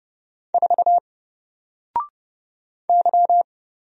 Morse code, heard as a keyed tone, about 30 wpm, 700 Hz. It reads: 4 Y
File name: Single-Letter-Number-30wpm